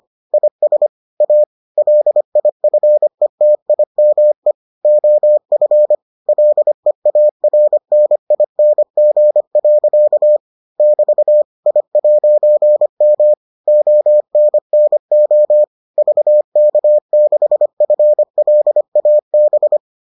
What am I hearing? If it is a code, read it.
Morse code, heard as a keyed tone, about 25 wpm, 600 Hz. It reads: IS A LIFETIME OF LEARNING. <BT> I'M ONNO VK6FLAB